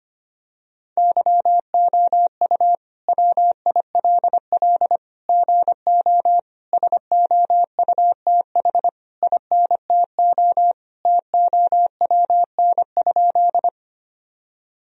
Morse code, heard as a keyed tone, about 25 wpm, 700 Hz. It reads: YOU WILL GO SOUTH INTO TOWN?
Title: Sentences from Top 1000 Words - Review Part 2 - 25wpm